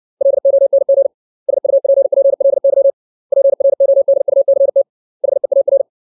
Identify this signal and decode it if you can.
Morse code, heard as a keyed tone, about 60 wpm, 550 Hz. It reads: LONG SUPPLY PRODUCT HEAR
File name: 632 words / 632 Words 60WPM